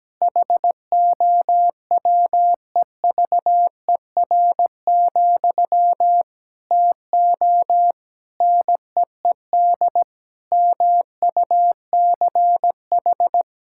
Morse code, heard as a keyed tone, about 17 wpm, 700 Hz. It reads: HOWEVER, TO NEED MUCH
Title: Sentences from Top 1000 Words - Review Part 6 - 17wpm